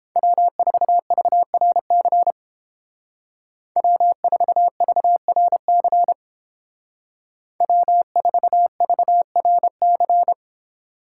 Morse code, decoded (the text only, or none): W4VRC W4VRC W4VRC